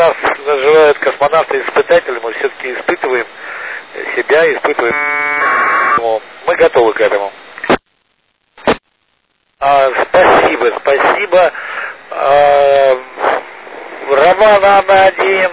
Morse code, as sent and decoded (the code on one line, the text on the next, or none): none